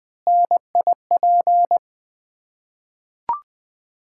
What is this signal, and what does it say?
Morse code, heard as a keyed tone, about 20 wpm, 700 Hz. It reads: NIP
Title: Ontario QSO Party Exchange 20wpm